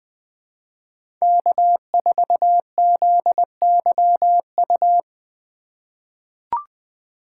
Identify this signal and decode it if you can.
Morse code, heard as a keyed tone, about 20 wpm, 700 Hz. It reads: K4ZYU
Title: USA Call Signs - 5 Characters - 4x Character Spacing - 20wpm